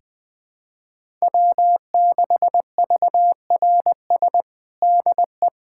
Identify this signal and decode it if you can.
Morse code, heard as a keyed tone, about 20 wpm, 700 Hz. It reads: W6VRS DE
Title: USA Callsign DE Callsign - 5 Characters - 20wpm